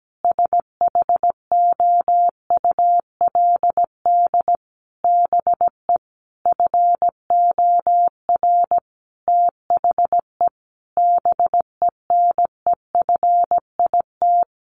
Morse code, heard as a keyed tone, about 17 wpm, 700 Hz. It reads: SHOULD BE FOR THE BENEFIT
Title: Sentences from Top 1300 Words 17wpm